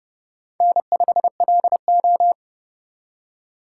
Morse code, heard as a keyed tone, about 27 wpm, 700 Hz. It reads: N5LO